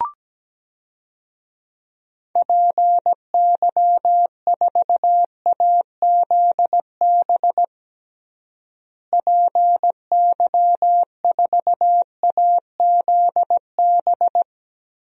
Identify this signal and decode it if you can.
Morse code, heard as a keyed tone, about 17 wpm, 700 Hz. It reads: PY4AZB PY4AZB
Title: DXCC-100 Call Signs - 6 Characters - Repeated 3x - 17wpm